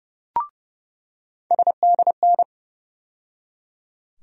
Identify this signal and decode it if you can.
Morse code, heard as a keyed tone, about 30 wpm, 700 Hz. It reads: SDN
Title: USA Call Sign Postfixes - 3 Letters 30wpm